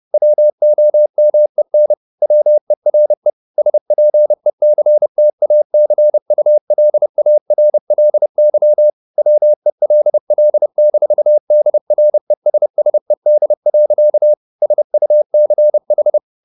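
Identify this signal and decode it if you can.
Morse code, heard as a keyed tone, about 30 wpm, 600 Hz. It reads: WOMEN WERE SPECTACULARLY WELL-DRESSED. SUCH
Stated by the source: Morse code at 30 wpm, the novel Never by Ken Follett in English, from CW med Gnister.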